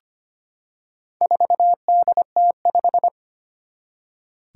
Morse code, heard as a keyed tone, about 25 wpm, 700 Hz. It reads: 4DT5
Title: Four Challenging Letter-Number Combinations 25wpm